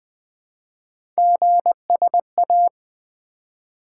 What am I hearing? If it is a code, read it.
Morse code, heard as a keyed tone, about 20 wpm, 700 Hz. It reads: GSA
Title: Three Letter-Number Combinations - 20wpm